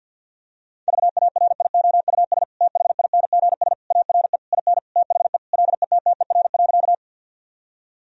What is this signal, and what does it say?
Morse code, heard as a keyed tone, about 50 wpm, 700 Hz. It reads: VARIOUS THINGS ARE IN THE LETTER.
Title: Sentences from Top 1000 Words - Review Part 4 - 50wpm